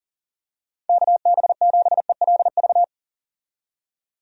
Morse code, heard as a keyed tone, about 40 wpm, 690 Hz. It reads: KB7ELV
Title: USA Call Signs - 6 Characters - 40wpm